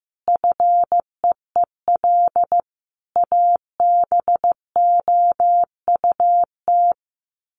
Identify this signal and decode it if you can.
Morse code, heard as a keyed tone, about 14 wpm, 700 Hz. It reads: FEEL ABOUT
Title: Sets of 5 Words 15wpm